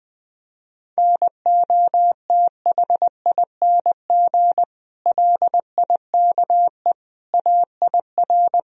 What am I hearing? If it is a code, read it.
Morse code, heard as a keyed tone, about 20 wpm, 700 Hz. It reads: NOTHING LIKE AIR
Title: Sentences from Top 1000 Words - Review Part 4 - 20wpm